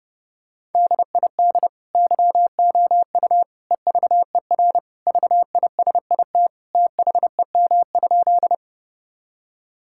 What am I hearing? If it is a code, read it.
Morse code, heard as a keyed tone, about 30 wpm, 700 Hz. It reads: DID YOU EVER VISIT THEM?
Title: Sentences from Top 1000 Words - Review Part 6 - 30wpm